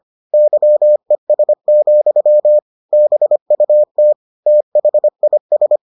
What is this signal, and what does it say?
Morse code, heard as a keyed tone, about 25 wpm, 600 Hz. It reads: YES, BUT THIS